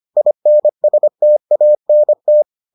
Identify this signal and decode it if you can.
Morse code, heard as a keyed tone, about 25 wpm, 600 Hz. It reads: INSTANT